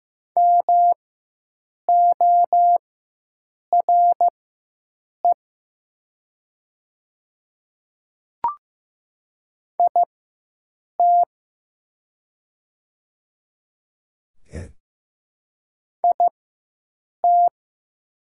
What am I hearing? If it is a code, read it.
Morse code, heard as a keyed tone, about 15 wpm, spaced as for 7 wpm, 700 Hz. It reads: MORE IT IT
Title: 4x Character Spacing - Top 100 Words - 15wpm